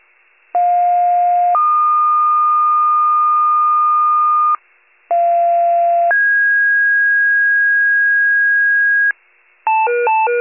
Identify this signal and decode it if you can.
Two-tone page, heard as a one-second tone, then a three-second tone, about 710 then 1220 Hz, about 690 then 1650 Hz, with noise. It is heard from 0.5 to 9.1 s.
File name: pager / 46 RESCUE / wcalert